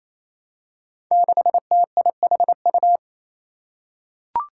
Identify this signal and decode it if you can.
Morse code, heard as a keyed tone, about 28 wpm, 700 Hz. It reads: 6TIHU
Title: Five Challenging Letter-Number Combinations 28wpm